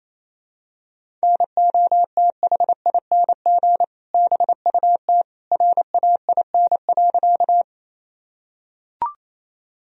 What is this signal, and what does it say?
Morse code, heard as a keyed tone, about 28 wpm, 700 Hz. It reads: NOTHING BUT RAIN.
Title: Sentences from Top 1000 Words - Review Part 1 - 28wpm